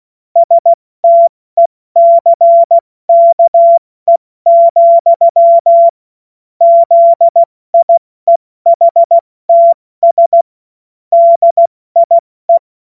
Morse code, heard as a keyed tone, about 16 wpm, 680 Hz. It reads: STECKE, ZIEHTS DIE
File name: Morsekurs / Witze / Witz 09 April 2024 70 F80